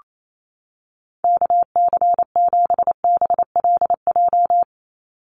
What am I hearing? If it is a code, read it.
Morse code, heard as a keyed tone, about 28 wpm, 700 Hz. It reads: KC7BLJ
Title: USA Call Signs 28wpm